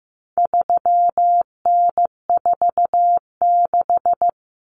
Morse code, heard as a keyed tone, about 15 wpm, 700 Hz. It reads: 3N46
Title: Common Cut Numbers - 4-Digit 15wpm